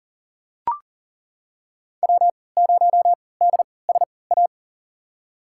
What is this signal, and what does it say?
Morse code, heard as a keyed tone, about 37 wpm, spaced as for 22 wpm, 700 Hz. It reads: W0DSA
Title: USA Call Signs - 5 Characters - 3x Character Spacing - 40wpm